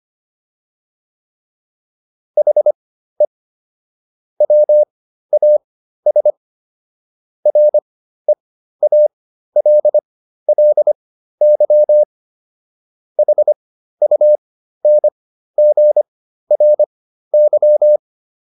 Morse code, heard as a keyed tone, about 25 wpm, spaced as for 13 wpm, 600 Hz. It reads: HE WAS REALLY HUNGRY